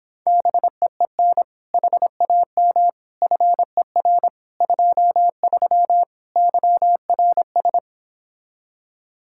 Morse code, heard as a keyed tone, about 29 wpm, 700 Hz. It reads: BEEN HAM FER 23 YRS